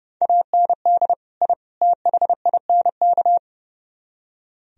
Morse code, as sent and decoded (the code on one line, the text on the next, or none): .- -. -.. / .. / - .... .. -. -.-
AND I THINK